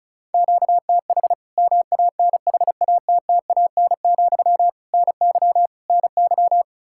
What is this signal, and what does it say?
Morse code, heard as a keyed tone, about 36 wpm, 700 Hz. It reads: QTH MANHATTAN, NY NY
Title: Practice QSO - Exchange 1 - Beginning - 35wpm